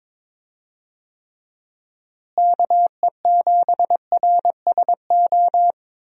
Morse code, heard as a keyed tone, about 22 wpm, 700 Hz. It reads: KE7RSO
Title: USA Call Signs - 6 Characters - Repeated 3x - 22wpm